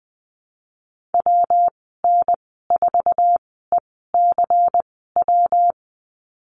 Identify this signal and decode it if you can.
Morse code, heard as a keyed tone, about 20 wpm, spaced as for 15 wpm, 700 Hz. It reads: WN4ECW
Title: 154 Lesson 25g - Letter p - Call Signs 20wpm